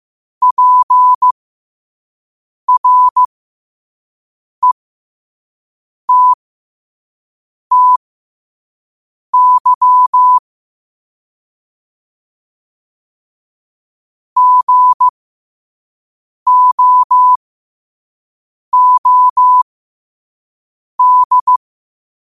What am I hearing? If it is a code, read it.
Morse code, heard as a keyed tone, about 15 wpm, spaced as for 5 wpm, 1000 Hz. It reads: PRETTY GOOD
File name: qotdfiles / 5wpm